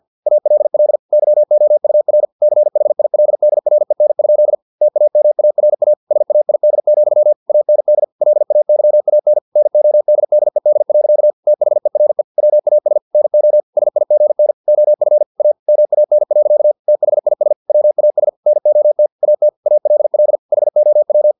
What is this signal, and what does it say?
Morse code, heard as a keyed tone, about 50 wpm, 600 Hz. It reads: ALL YOUR CHILDREN? TAMARA SAID, AND LAYAN NODDED. THERE WAS NO SIGN OF A MAN. THIS WAS NOT AT ALL HOW
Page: Never 49wpm